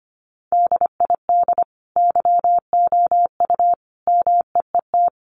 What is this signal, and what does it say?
Morse code, heard as a keyed tone, about 25 wpm, 700 Hz. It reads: DID YOU MEET
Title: Sentences from Top 1100 Words 25wpm